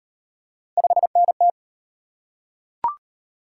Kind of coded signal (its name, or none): Morse code